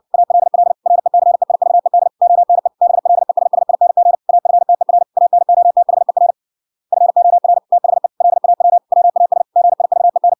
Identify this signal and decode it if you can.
Morse code, heard as a keyed tone, about 60 wpm, 700 Hz. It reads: ALL RECEIVED ONE BLESSING AFTER ANOTHER FOR THE LAW WAS GIVEN